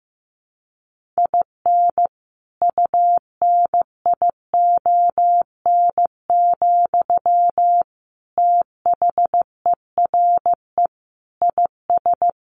Morse code, heard as a keyed tone, about 15 wpm, 700 Hz. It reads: IN UNION, THERE IS
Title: Sentences from Top 1400 Words 15wpm